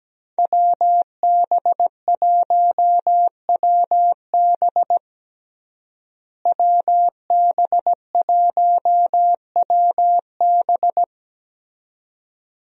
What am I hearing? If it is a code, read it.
Morse code, heard as a keyed tone, about 17 wpm, 700 Hz. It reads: WB1WB WB1WB
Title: USA Call Signs - 5 Characters - Repeated 2x - 17wpm